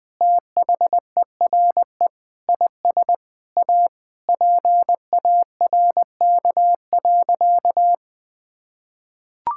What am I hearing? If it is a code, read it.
Morse code, heard as a keyed tone, about 19 wpm, 700 Hz. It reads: THERE IS A PARK.
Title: Sentences with 5-Word Phrases - Spoken Beginning - 20wpm